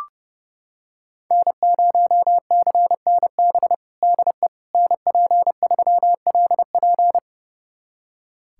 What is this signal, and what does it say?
Morse code, heard as a keyed tone, about 30 wpm, 700 Hz. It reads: N0CNB DE NP3LP
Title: USA Callsign DE Callsign - 5 Characters - 30wpm